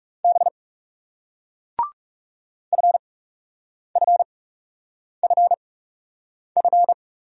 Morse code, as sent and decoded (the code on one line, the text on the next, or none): -... / ..-. ..-. ..-. ..-.
B FFFF